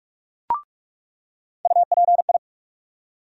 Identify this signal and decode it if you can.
Morse code, heard as a keyed tone, about 46 wpm, 700 Hz. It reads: UPI
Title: USA Call Sign Postfixes - 3 Letters 45wpm